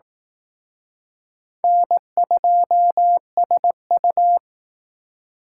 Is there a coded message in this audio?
Yes: Morse code from 1.6 to 4.4 s, heard as a keyed tone, about 19 wpm, 700 Hz. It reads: N2SU